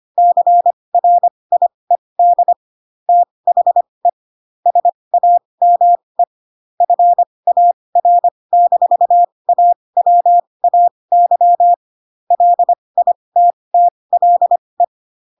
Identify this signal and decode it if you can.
Morse code, heard as a keyed tone, about 25 wpm, spaced as for 20 wpm, 700 Hz. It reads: CRIED THE SAME FAR-AWAY LITTLE
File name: Pinocchio / Pinocchio ch1 15 25 ik1hge